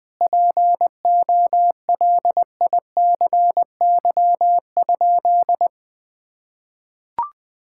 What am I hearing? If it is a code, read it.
Morse code, heard as a keyed tone, about 21 wpm, 700 Hz. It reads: POLICY?
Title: Sentences from Top 1000 Words - Review Part 5 - 20wpm